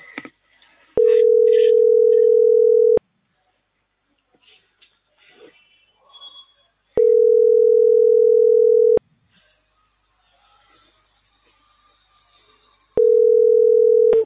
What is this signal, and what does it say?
Ringback tone (North America), 440 + 480 Hz.